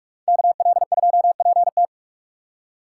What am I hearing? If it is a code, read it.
Morse code, heard as a keyed tone, about 43 wpm, 690 Hz. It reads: KL2PT
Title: USA Call Signs - 4x Character Spacing - 45wpm